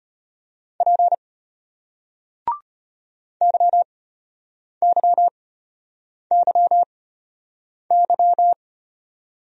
Morse code, heard as a keyed tone, about 34 wpm, spaced as for 8 wpm, 700 Hz. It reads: P YYYY